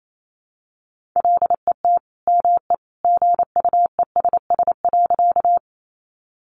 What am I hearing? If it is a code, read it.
Morse code, heard as a keyed tone, about 28 wpm, 700 Hz. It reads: LET ME GUESS.
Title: Sentences from Top 1500 Words 28wpm